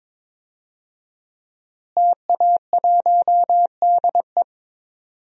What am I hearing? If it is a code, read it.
Morse code, heard as a keyed tone, about 21 wpm, 700 Hz. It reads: TA1DE
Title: DXCC-100 Call Signs - 5 Characters - Repeated 2x - 22wpm